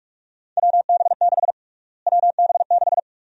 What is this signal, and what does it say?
Morse code, heard as a keyed tone, about 41 wpm, 700 Hz. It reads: WB6 WB6
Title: USA Call Sign Prefixes - Repeated 2x - 45wpm